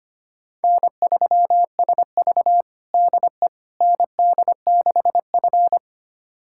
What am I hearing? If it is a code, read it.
Morse code, heard as a keyed tone, about 25 wpm, 700 Hz. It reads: N3SV DE ND6F
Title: USA Callsign DE Callsign - 4 Characters - 25wpm